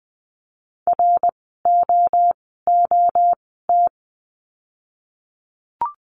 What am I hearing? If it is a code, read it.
Morse code, heard as a keyed tone, about 20 wpm, spaced as for 14 wpm, 700 Hz. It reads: ROOT
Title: 054 Lesson 10f - Letter r - Words 20wpm